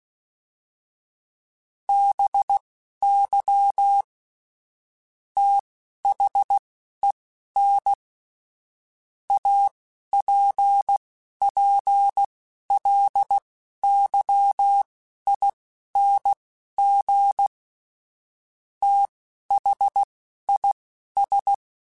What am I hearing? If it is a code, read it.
Morse code, heard as a keyed tone, about 16 wpm, spaced as for 11 wpm, 800 Hz. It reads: BY THEN APPLYING THIS
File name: Technical Subjects / Technical Subjects-mp3s / Sw Pwr Sup @10